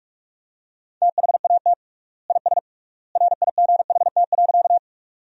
Morse code, heard as a keyed tone, about 46 wpm, 700 Hz. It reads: THAT IS RIGHT.
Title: 3x Word-Spacing - Sentences from Top 200 Words - 45wpm